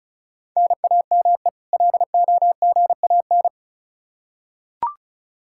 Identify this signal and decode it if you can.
Morse code, heard as a keyed tone, about 36 wpm, 700 Hz. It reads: NAME LOGAN
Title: Practice QSO - Exchange 1 - Beginning - 35wpm